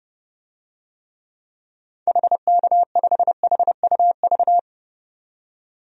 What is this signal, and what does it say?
Morse code, heard as a keyed tone, about 30 wpm, 700 Hz. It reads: HK5HUV
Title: DXCC-100 Call Signs - 6 Characters - 30wpm